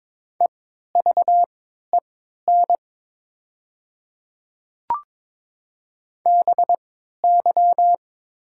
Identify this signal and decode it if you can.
Morse code, heard as a keyed tone, about 21 wpm, spaced as for 12 wpm, 700 Hz. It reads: EVEN BY
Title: 3x Character Spacing - Top 200 Words - 22wpm